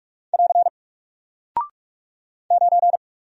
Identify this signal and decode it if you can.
Morse code, heard as a keyed tone, about 46 wpm, 700 Hz. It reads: <AR> 9
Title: Single Character (Letter:Number:Punctuation:Prosign) Alt 45wpm